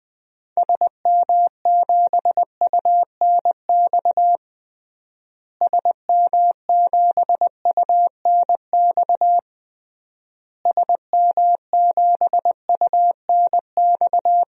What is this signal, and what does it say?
Morse code, heard as a keyed tone, about 20 wpm, 700 Hz. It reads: SM7UNX SM7UNX SM7UNX
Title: DXCC-100 Call Signs - 6 Characters - Repeated 3x - 20wpm